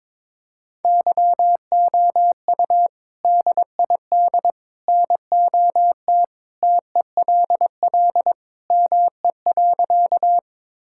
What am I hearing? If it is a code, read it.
Morse code, heard as a keyed tone, about 22 wpm, 700 Hz. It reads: YOU DID NOT TELL ME.
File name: practice / 22wpm / Sentences with 3-Word Phrases - 22wpm